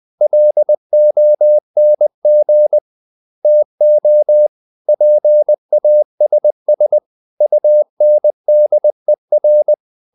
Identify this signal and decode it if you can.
Morse code, heard as a keyed tone, about 20 wpm, 600 Hz. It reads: LONG TO PASS UNDER
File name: the old man and the sea part 2 20wpm